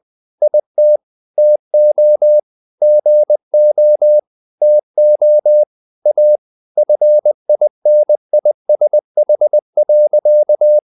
Morse code, heard as a keyed tone, about 20 wpm, 600 Hz. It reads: IT TO GO TO A FINISH.